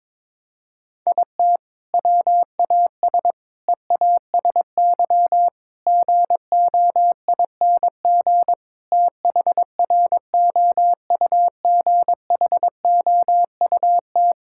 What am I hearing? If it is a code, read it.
Morse code, heard as a keyed tone, about 22 wpm, 700 Hz. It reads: IT WAS EASY GOING THROUGHOUT
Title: Sentences from Top 1000 Words - Review Part 5 - 22wpm